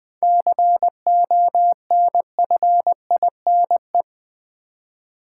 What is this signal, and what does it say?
Morse code, heard as a keyed tone, about 20 wpm, 700 Hz. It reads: CONFINE
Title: 7 Letter Words - First 3 Letters Given - 20wpm